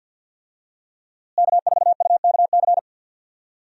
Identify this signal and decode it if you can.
Morse code, heard as a keyed tone, about 49 wpm, 700 Hz. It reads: K4UKC
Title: USA Call Signs - 5 Characters - Repeated 3x - 50wpm